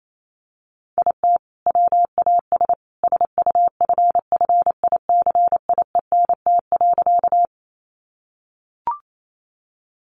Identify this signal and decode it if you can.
Morse code, heard as a keyed tone, about 28 wpm, 700 Hz. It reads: IT WAS SUFFICIENT.
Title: Sentences from Top 1400 Words 28wpm